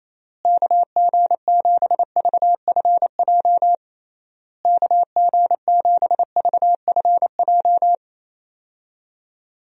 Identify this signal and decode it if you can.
Morse code, heard as a keyed tone, about 28 wpm, 700 Hz. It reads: KG7VFJ KG7VFJ
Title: USA Call Signs - 6 Characters - Repeated 3x - 28wpm